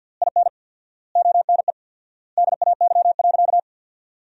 Morse code, heard as a keyed tone, about 53 wpm, 700 Hz. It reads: IN ONE DAY.
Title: 4x Word-Spacing - Sentences from Top 500 Words - 50wpm